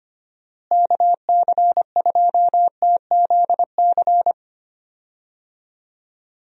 Morse code, heard as a keyed tone, about 25 wpm, 700 Hz. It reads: KC2TZC
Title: USA Call Signs - 6 Characters - Repeated 2x - 25wpm